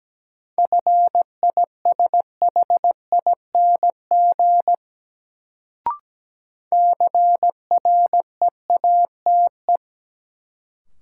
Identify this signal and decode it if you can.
Morse code, heard as a keyed tone, about 17 wpm, 700 Hz. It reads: FISHING CREATE
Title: Sentences from Top 1600 Words - New Word Drill Alt - 17wpm